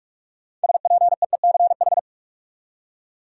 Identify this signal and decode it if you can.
Morse code, heard as a keyed tone, about 46 wpm, 700 Hz. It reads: SPEECH
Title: Six Letter Words - 45wpm